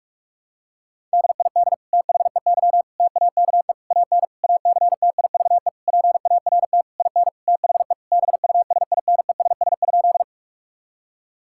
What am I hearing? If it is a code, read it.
Morse code, heard as a keyed tone, about 46 wpm, 700 Hz. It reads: DID THEY TAKE AN ACTIVE PART IN THE BUSINESS?
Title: Sentences from Top 1000 Words - Review Part 6 - 45wpm